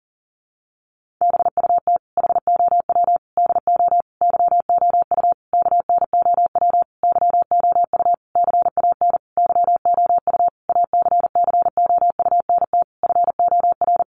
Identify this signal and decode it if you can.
Morse code, heard as a keyed tone, about 37 wpm, 700 Hz. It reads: BUT HOW DO YOU KNOW YOU CAN YOU ACCOUNT FOR
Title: Sentences from Top 900 Words 40wpm